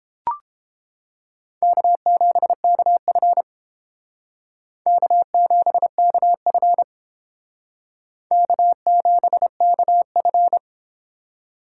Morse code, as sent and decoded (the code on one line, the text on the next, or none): -.- --... -.- ..-. / -.- --... -.- ..-. / -.- --... -.- ..-.
K7KF K7KF K7KF